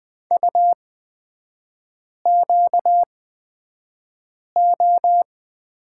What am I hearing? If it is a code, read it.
Morse code, heard as a keyed tone, about 20 wpm, spaced as for 5 wpm, 690 Hz. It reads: UQO
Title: 236 Lesson 38e - Letter q - Three Characters 20wpm